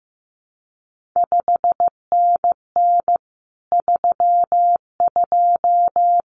Morse code, heard as a keyed tone, about 15 wpm, 700 Hz. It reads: 5NN 32
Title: CQ WW DX Contest Exchange Encore - 15wpm